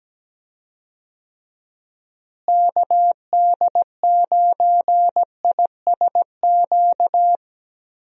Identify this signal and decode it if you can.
Morse code, heard as a keyed tone, about 17 wpm, 700 Hz. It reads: KD9ISQ
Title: USA Call Signs - 6 Characters - Repeated 3x - 17wpm